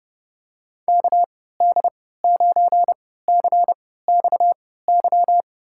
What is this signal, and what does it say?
Morse code, heard as a keyed tone, about 30 wpm, spaced as for 17 wpm, 700 Hz. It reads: KD9CXY
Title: USA Call Signs - 6 Characters - 3x Character Spacing - 30wpm